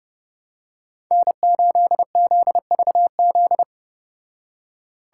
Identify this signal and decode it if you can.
Morse code, heard as a keyed tone, about 30 wpm, 700 Hz. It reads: N8ZVZ